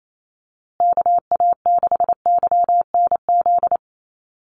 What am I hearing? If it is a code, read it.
Morse code, heard as a keyed tone, about 28 wpm, 700 Hz. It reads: KA6YNZ
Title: USA Call Signs Repeated 2x 28wpm